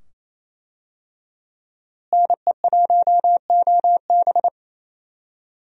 Morse code, heard as a keyed tone, about 28 wpm, 700 Hz. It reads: NE1OB
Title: USA Call Signs - 5 Characters - Repeated 2x - 28wpm